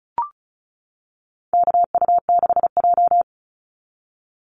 Morse code, heard as a keyed tone, about 34 wpm, 700 Hz. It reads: KU6J
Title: NA SOTA Chaser Callsigns 35wpm